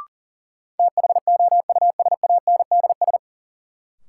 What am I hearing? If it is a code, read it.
Morse code, heard as a keyed tone, about 40 wpm, 700 Hz. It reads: THOUSANDS
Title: Sets of 4 Words - New Word Drill - 40wpm